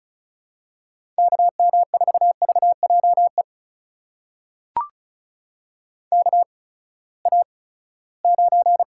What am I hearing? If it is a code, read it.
Morse code, heard as a keyed tone, about 36 wpm, spaced as for 20 wpm, 700 Hz. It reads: KM4VJE KA9
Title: USA Call Signs - 6 Characters - 8x Character Spacing - 35wpm